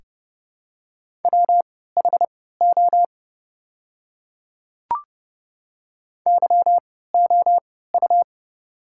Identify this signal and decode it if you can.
Morse code, heard as a keyed tone, about 29 wpm, spaced as for 17 wpm, 700 Hz. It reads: WHO YOU
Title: 3x Character Spacing - Top 100 Words - 30wpm